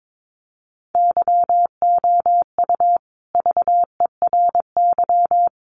Morse code, heard as a keyed tone, about 22 wpm, 700 Hz. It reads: YOU VERY
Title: Sets of 5 Words Encore 22wpm